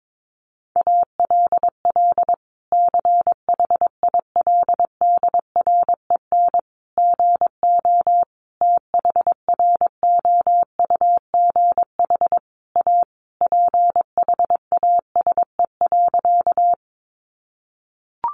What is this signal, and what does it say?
Morse code, heard as a keyed tone, about 22 wpm, 700 Hz. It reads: ALL CHILDREN GO THROUGH A PHASE.
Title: Sentences from Top 1200 Words 22wpm